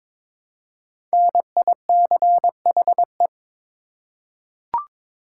Morse code, heard as a keyed tone, about 22 wpm, 700 Hz. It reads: NICHE
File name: 5 Letter Words - First 3 Letters Given - 22wpm